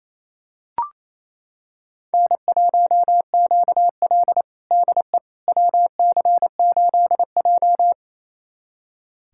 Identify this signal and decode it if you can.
Morse code, heard as a keyed tone, about 28 wpm, 700 Hz. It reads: N1QL DE WC8J
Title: USA Callsign DE Callsign - 4 Characters - 28wpm